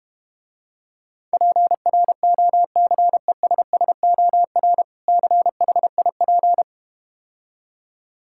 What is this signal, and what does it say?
Morse code, heard as a keyed tone, about 32 wpm, 700 Hz. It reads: PROCESSOR CHIP